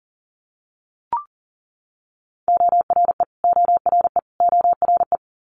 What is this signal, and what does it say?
Morse code, heard as a keyed tone, about 38 wpm, 700 Hz. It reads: ORE ORE ORE
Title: Multi-letter Phonemes Repeated 3x 40wpm